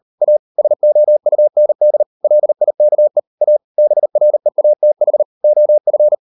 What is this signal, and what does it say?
Morse code, heard as a keyed tone, about 38 wpm, 600 Hz. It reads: A SOUND LIKE A BREATH OF